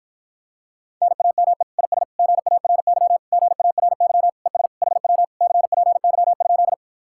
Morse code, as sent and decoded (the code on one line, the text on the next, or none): -. .- -- . / .. ... / --. .- .-. -.-- / --. .- .-. -.-- / . ... / .... .-- / -.-. .--. -.-- ..--..
NAME IS GARY GARY ES HW CPY?